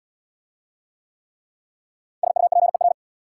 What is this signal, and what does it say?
Morse code, heard as a keyed tone, about 74 wpm, 690 Hz. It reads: SUPER